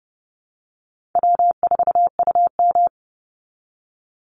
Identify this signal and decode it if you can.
Morse code, heard as a keyed tone, about 30 wpm, 700 Hz. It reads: W4UM
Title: Easy Call Signs Repeated 3x - 30wpm